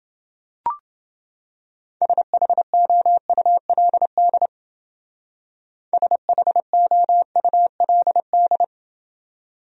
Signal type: Morse code

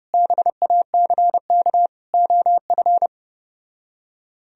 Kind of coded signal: Morse code

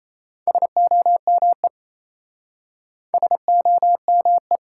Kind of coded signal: Morse code

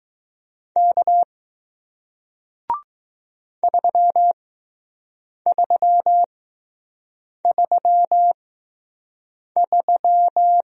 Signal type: Morse code

